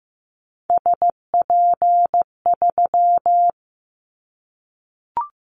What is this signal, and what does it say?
Morse code, heard as a keyed tone, about 15 wpm, 700 Hz. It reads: SP3
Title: Callsign Prefixes for Top 25 DXCC - Repeated 3x - 15wpm